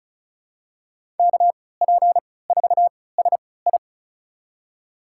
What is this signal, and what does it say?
Morse code, heard as a keyed tone, about 36 wpm, spaced as for 20 wpm, 700 Hz. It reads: KP4SI
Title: USA Call Signs - 3x Character Spacing - 35wpm